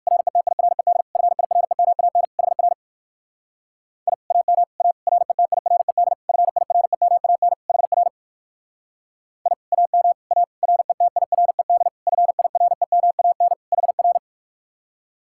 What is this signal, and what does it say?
Morse code, heard as a keyed tone, about 56 wpm, 700 Hz. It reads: RETIRED FIREMAN HR I AM A RETIRED FIREMAN HR I AM A RETIRED FIREMAN HR
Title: Speed-Racing Common QSO Phrases - 45wpm